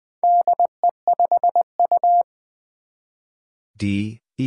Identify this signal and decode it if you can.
Morse code, heard as a keyed tone, about 20 wpm, 700 Hz. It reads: DE5U
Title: Four Challenging Letter-Number Combinations 20wpm